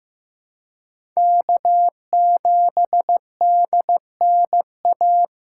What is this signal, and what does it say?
Morse code, heard as a keyed tone, about 15 wpm, 700 Hz. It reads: K7DNA
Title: USA Call Signs - 5 Characters - 2x Character Spacing - 15wpm